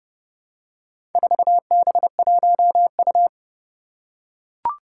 Morse code, heard as a keyed tone, about 28 wpm, 690 Hz. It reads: 4B1U
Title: Four Challenging Letter-Number Combinations 30wpm